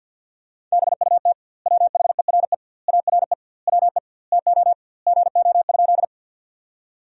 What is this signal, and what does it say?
Morse code, heard as a keyed tone, about 50 wpm, 700 Hz. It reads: BUT WHERE ARE WE TO GO?
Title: 2x Word-Spacing - Sentences from Top 200 Words - 50wpm